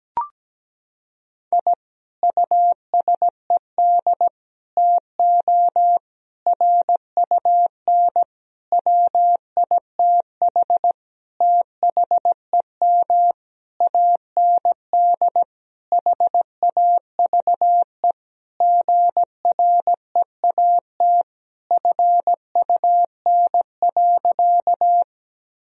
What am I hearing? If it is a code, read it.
Morse code, heard as a keyed tone, about 17 wpm, 700 Hz. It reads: I USED TO RUN WITH THEM AND HAVE GREAT FUN.
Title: Sentences from Top 1000 Words - Review Part 4 - 17wpm